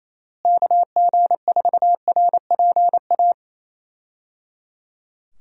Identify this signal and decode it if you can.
Morse code, heard as a keyed tone, about 27 wpm, 700 Hz. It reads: KG4RPA
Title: USA Call Signs - 6 Characters - Repeated 3x - 28wpm